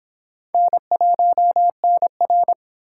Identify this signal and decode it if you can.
Morse code, heard as a keyed tone, about 29 wpm, 700 Hz. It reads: N1NR